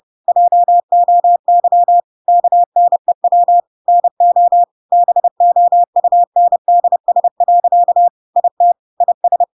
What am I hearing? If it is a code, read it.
Morse code, heard as a keyed tone, about 30 wpm, 700 Hz. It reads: JOY KNEW NO BOUNDS. IT IS
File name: Pinocchio / Pinocchio ch5 30 30 ik1hge